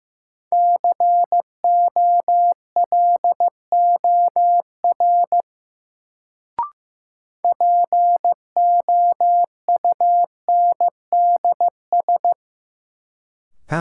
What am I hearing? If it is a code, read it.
Morse code, heard as a keyed tone, about 15 wpm, 700 Hz. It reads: COLOR POUNDS
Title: Sentences from Top 1000 Words - Review Part 2 - New Word Drill - 15wpm